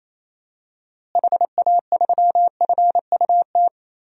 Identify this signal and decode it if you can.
Morse code, heard as a keyed tone, about 28 wpm, 700 Hz. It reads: HA3FUT
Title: DXCC-100 Call Signs - 6 Characters - Repeated 2x - 28wpm